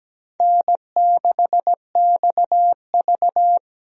Morse code, heard as a keyed tone, about 17 wpm, 700 Hz. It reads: N6XV
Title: USA Call Signs - 4 Characters - Repeated 3x - 17wpm